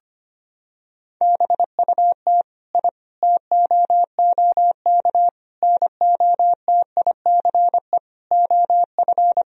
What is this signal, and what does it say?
Morse code, heard as a keyed tone, about 25 wpm, 700 Hz. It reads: BUT I TOOK NOTICE OF
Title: Sentences from Top 1000 Words - Review Part 2 - 25wpm